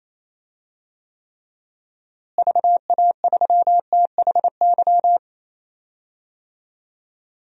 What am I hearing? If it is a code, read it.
Morse code, heard as a keyed tone, about 28 wpm, 700 Hz. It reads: VA3THY
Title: DXCC-100 Call Signs - 6 Characters - Repeated 3x - 28wpm